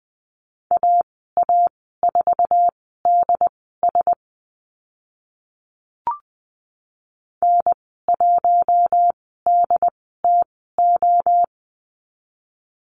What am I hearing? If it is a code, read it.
Morse code, heard as a keyed tone, about 20 wpm, spaced as for 14 wpm, 700 Hz. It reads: AA4DS N1DTO
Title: 082 Lesson 14g - Number 2 - Call Signs 20wpm